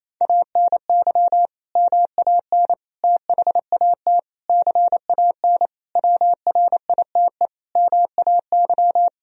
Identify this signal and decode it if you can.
Morse code, heard as a keyed tone, about 28 wpm, 700 Hz. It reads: ANY MAN THAT CAN WRITE MAY
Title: Sentences from Top 1000 Words - Review Part 1 - 28wpm